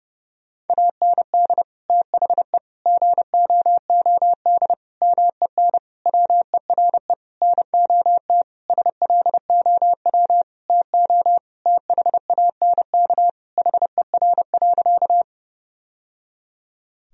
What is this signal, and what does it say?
Morse code, heard as a keyed tone, about 30 wpm, 700 Hz. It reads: AND THE GOOD MEN WERE NOT SLOW TO THANK HER.
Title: Sentences from Top 1000 Words - Review Part 2 - 30wpm